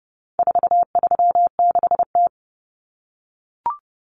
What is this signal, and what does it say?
Morse code, heard as a keyed tone, about 30 wpm, 700 Hz. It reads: 436T
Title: Common Cut Numbers - 4-Digit 30wpm